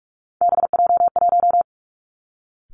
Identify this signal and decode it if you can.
Morse code, heard as a keyed tone, about 46 wpm, 700 Hz. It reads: BJ1
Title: Three Challenging Letter-Number Combinations 45wpm